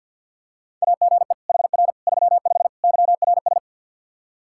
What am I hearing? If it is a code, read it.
Morse code, heard as a keyed tone, about 50 wpm, 700 Hz. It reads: AGE HR 35 YRS
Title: K1USN SST QSO Practice 50wpm